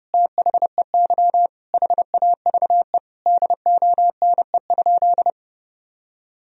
Morse code, heard as a keyed tone, about 30 wpm, 700 Hz. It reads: THEY HAVE DONE?
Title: Sentences from Top 1000 Words - Review Part 3 - 30wpm